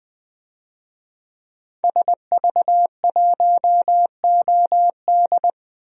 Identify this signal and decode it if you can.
Morse code, heard as a keyed tone, about 20 wpm, 700 Hz. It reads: SV1OD
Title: DXCC-100 Call Signs - 5 Characters - 20wpm